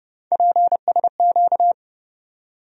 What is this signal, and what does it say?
Morse code, heard as a keyed tone, about 31 wpm, 700 Hz. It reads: PSQ
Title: USA Call Sign Postfixes - 3 Letters 30wpm